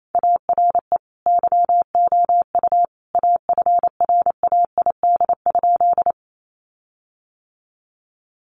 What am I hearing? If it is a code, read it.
Morse code, heard as a keyed tone, about 28 wpm, 700 Hz. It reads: ARE YOU AFRAID?
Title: Sentences from Top 1500 Words 28wpm